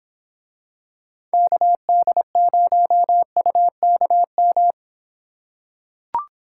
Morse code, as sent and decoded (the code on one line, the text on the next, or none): -.- -.. ----- ..- -.- --
KD0UKM